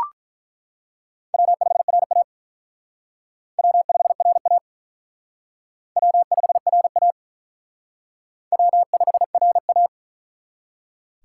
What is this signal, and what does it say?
Morse code, heard as a keyed tone, about 48 wpm, 700 Hz. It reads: W5RA W5RA W5RA W5RA